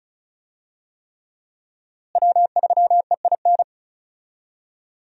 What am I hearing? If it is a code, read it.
Morse code, heard as a keyed tone, about 36 wpm, 700 Hz. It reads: W3EIN
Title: USA Call Signs - 5 Characters - Repeated 3x - 35wpm